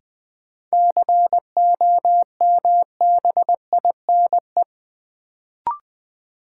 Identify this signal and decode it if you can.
Morse code, heard as a keyed tone, about 20 wpm, 700 Hz. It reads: COMBINE
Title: Sentences from Top 1600 Words - New Word Drill - 20wpm